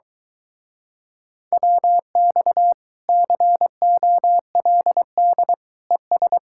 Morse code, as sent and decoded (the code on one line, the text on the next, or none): .-- -..- / -.-. --- .-.. -.. / . ...
WX COLD ES